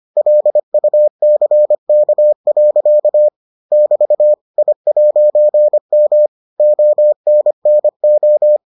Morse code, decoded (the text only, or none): LUCK. <BT> I'M ONNO